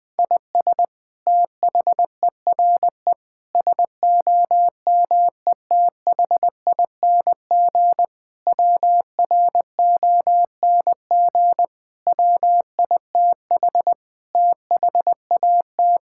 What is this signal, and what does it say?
Morse code, heard as a keyed tone, about 21 wpm, 700 Hz. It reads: IS THERE SOMETHING WRONG WITH THAT
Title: Sentences from Top 1000 Words - Review Part 6 - 20wpm